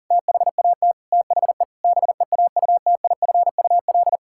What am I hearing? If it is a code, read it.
Morse code, heard as a keyed tone, about 40 wpm, 700 Hz. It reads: THAT THE BEAUTIFUL
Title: Pangram Challenge v2 - 40wpm